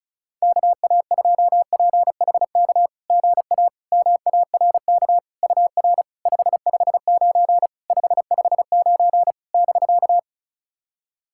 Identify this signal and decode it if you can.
Morse code, heard as a keyed tone, about 36 wpm, 690 Hz. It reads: KA2PHK GA MARK UR 559 559 <BK>
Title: SOTA QSO Practice 35wpm